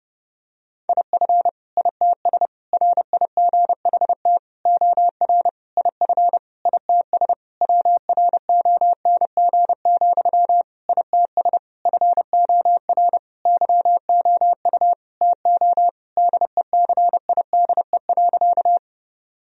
Morse code, heard as a keyed tone, about 30 wpm, 700 Hz. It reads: IF ITS RIGHT OR IF ITS WRONG, ITS FOR YOU TO DECIDE.
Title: Sentences from Top 1000 Words - Review Part 3 - 30wpm